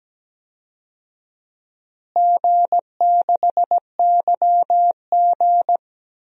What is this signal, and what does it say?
Morse code, heard as a keyed tone, about 17 wpm, 700 Hz. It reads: G6YG
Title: DXCC-100 Call Signs - 4 Characters - Repeated 2x - 17wpm